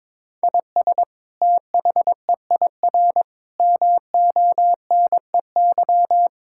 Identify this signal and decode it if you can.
Morse code, heard as a keyed tone, about 22 wpm, 700 Hz. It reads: IS THEIR MONEY
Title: Sentences from Top 1600 Words 22wpm